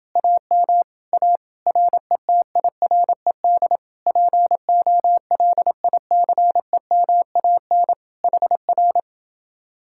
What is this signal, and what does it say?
Morse code, heard as a keyed tone, about 30 wpm, 700 Hz. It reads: AM A RETIRED POLICEMAN HR